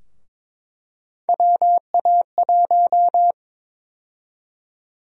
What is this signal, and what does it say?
Morse code, heard as a keyed tone, about 22 wpm, 700 Hz. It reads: WA1
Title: USA Call Sign Prefixes - Repeated 2x - 22wpm